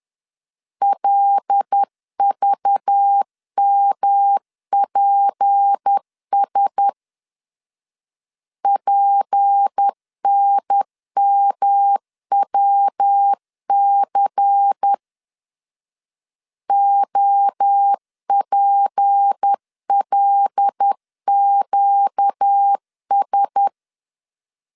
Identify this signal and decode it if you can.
Morse code, heard as a keyed tone, about 10 wpm, 800 Hz. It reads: LVMPS PNMWC OPLQS
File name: LEZIONI / 6 PKX0 / 6 P 10wpm 2432